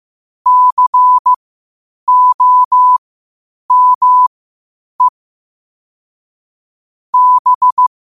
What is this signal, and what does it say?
Morse code, heard as a keyed tone, about 15 wpm, spaced as for 8 wpm, 1000 Hz. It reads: COME B
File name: qotdfiles / 8wpm